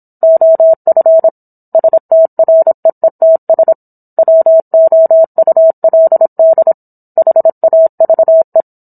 Morse code, heard as a keyed tone, about 27 wpm, 650 Hz. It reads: OF STREETS WOULD HAVE